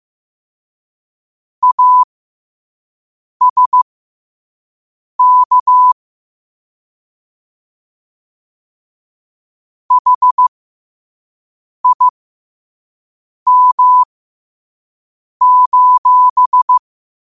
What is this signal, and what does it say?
Morse code, heard as a keyed tone, about 15 wpm, spaced as for 5 wpm, 1000 Hz. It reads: ASK HIM:
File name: qotdfiles / 5wpm